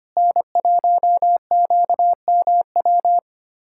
Morse code, heard as a keyed tone, about 25 wpm, 700 Hz. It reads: N1QMW
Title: USA Callsign DE Callsign 25wpm